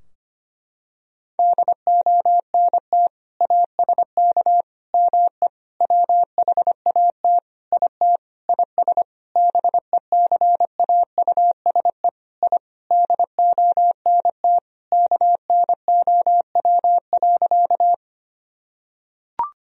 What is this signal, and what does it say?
Morse code, heard as a keyed tone, about 25 wpm, 700 Hz. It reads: DONT ASK ME WHAT IT IS BECAUSE I DONT KNOW.
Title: Sentences from Top 1000 Words - Review Part 3 - 25wpm